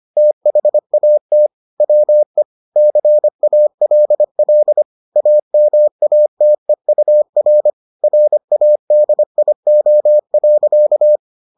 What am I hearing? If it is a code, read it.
Morse code, heard as a keyed tone, about 25 wpm, 600 Hz. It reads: THAT WE CALL AMATEUR RADIO.